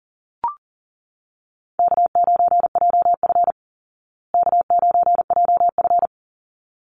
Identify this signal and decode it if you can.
Morse code, heard as a keyed tone, about 40 wpm, 700 Hz. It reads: K9JF K9JF
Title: Easy Call Signs Repeated 2x - 40wpm